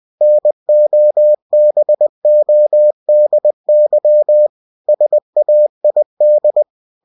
Morse code, heard as a keyed tone, about 20 wpm, 600 Hz. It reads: NOBODY SAID
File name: cw-books / winnie-the-pooh / winnie-20wpm-page-0064